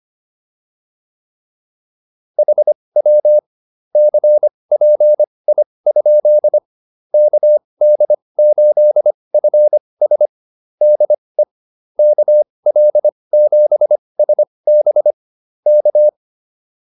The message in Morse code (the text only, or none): HW CPI? KD8FS DE KL7SB K